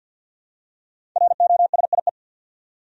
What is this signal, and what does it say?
Morse code, heard as a keyed tone, about 46 wpm, 700 Hz. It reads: ROSIE
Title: USA Common Names - 50wpm